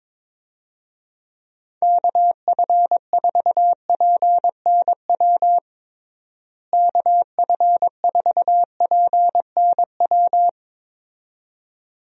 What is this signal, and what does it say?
Morse code, heard as a keyed tone, about 22 wpm, 700 Hz. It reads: KF4PNW KF4PNW
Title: USA Call Signs - 6 Characters - Repeated 2x - 22wpm